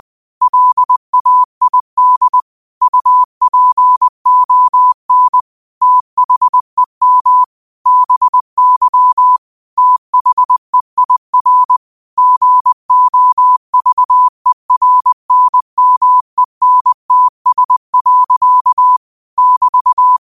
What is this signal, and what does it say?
Morse code, heard as a keyed tone, about 20 wpm, 1000 Hz. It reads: LAID UPON THEM BY THEIR GOVERNMENTS. <BT>
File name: qotdfiles / 20wpm